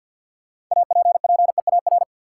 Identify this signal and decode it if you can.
Morse code, heard as a keyed tone, about 54 wpm, 700 Hz. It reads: APPEAR
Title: Sentences from Top 1200 Words - New Word Drill Alt - 50wpm